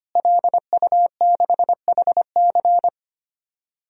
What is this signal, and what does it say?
Morse code, heard as a keyed tone, about 25 wpm, 700 Hz. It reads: LU6HC
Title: DXCC-100 Call Signs - 5 Characters - Repeated 2x - 25wpm